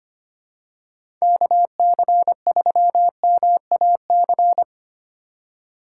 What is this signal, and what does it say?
Morse code, heard as a keyed tone, about 25 wpm, 700 Hz. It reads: KC3MAC
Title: USA Call Signs - 6 Characters - 3x Character Spacing - 25wpm